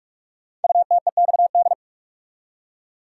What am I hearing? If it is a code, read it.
Morse code, heard as a keyed tone, about 44 wpm, 690 Hz. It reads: UTEXD
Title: Code Groups - 5 Letters - 45wpm